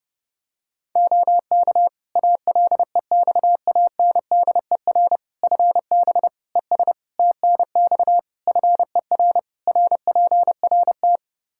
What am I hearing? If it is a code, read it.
Morse code, heard as a keyed tone, about 30 wpm, 700 Hz. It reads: OK ALEXANDER FB ES TNX FER RPRT
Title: Practice QSO - Exchange 2 - Info - 30wpm